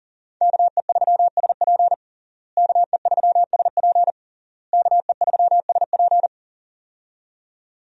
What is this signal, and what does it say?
Morse code, heard as a keyed tone, about 40 wpm, 700 Hz. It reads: KE3SP KE3SP KE3SP
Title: USA Call Signs - 5 Characters - Repeated 3x - 40wpm